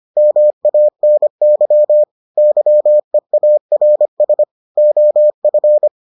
Morse code, heard as a keyed tone, about 25 wpm, 600 Hz. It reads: MANY YEARS OF